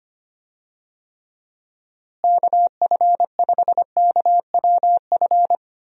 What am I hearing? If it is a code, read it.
Morse code, heard as a keyed tone, about 25 wpm, 700 Hz. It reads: KF5KWF
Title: USA Call Signs - 6 Characters - Repeated 2x - 25wpm